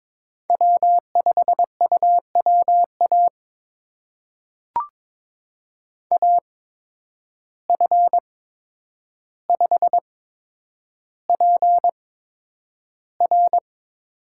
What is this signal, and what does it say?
Morse code, heard as a keyed tone, about 22 wpm, spaced as for 9 wpm, 700 Hz. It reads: W5UWA AF5PR
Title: USA Call Signs - 5 Characters - 8x Character Spacing - 22wpm